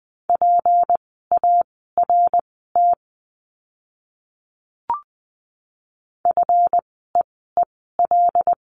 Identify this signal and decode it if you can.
Morse code, heard as a keyed tone, about 20 wpm, spaced as for 15 wpm, 700 Hz. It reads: PART FEEL
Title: 206 Lesson 33f - Letter j - Words 20wpm